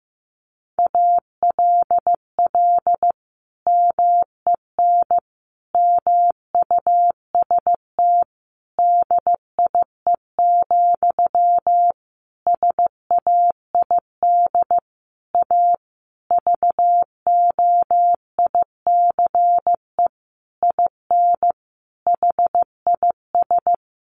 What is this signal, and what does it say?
Morse code, heard as a keyed tone, about 15 wpm, 700 Hz. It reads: ALL MEN MUST DIE, SAID A VOICE IN HIS
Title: Sentences from Top 1300 Words 15wpm